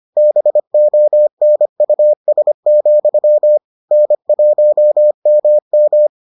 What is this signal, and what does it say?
Morse code, heard as a keyed tone, about 25 wpm, 600 Hz. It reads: BONUS, N1MM